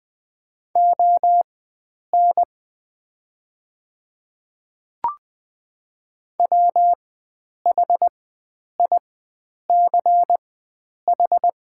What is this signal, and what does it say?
Morse code, heard as a keyed tone, about 20 wpm, spaced as for 9 wpm, 700 Hz. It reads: ON WHICH
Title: 4x Character Spacing - Top 100 Words - 20wpm